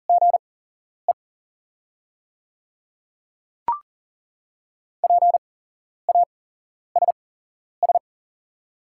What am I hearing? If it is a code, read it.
Morse code, heard as a keyed tone, about 39 wpm, spaced as for 11 wpm, 690 Hz. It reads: GE PASS
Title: 8x Character Spacing - Top 500 Words - 40wpm